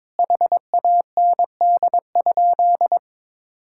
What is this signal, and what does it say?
Morse code, heard as a keyed tone, about 22 wpm, 700 Hz. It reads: HAND?
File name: Sentences with 9-Word Phrases - 22wpm